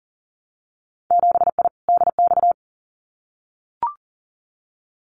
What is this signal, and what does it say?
Morse code, heard as a keyed tone, about 40 wpm, 700 Hz. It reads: 7I DX
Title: Winter Field Day Exchanges Encore 40wpm